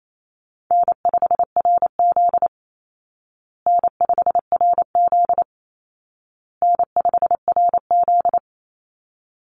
Morse code, heard as a keyed tone, about 28 wpm, 700 Hz. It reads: N5RZ N5RZ N5RZ
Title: Call Signs Repeated 2x - 28wpm